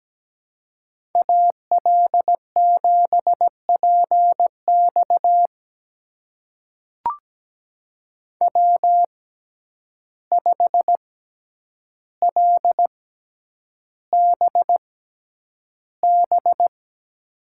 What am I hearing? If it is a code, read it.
Morse code, heard as a keyed tone, about 17 wpm, spaced as for 9 wpm, 700 Hz. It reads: AL7PX W5LBB
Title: USA Call Signs - 5 Characters - 6x Character Spacing - 17wpm